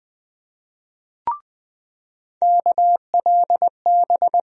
Morse code, heard as a keyed tone, about 20 wpm, 700 Hz. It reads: KLB
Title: USA Call Sign Postfixes - 3 Letters 20wpm